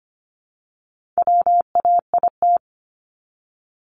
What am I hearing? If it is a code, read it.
Morse code, heard as a keyed tone, about 25 wpm, 700 Hz. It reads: WAIT
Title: ING Suffix - Top 100 Words with Base Words 25wpm